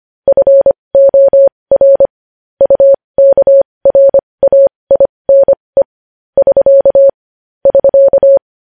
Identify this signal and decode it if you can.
Morse code, heard as a keyed tone, about 25 wpm, spaced as for 20 wpm, 550 Hz. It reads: FOR UKRAINE <SK> <SK>
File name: podcast / News headlines-1x-25x20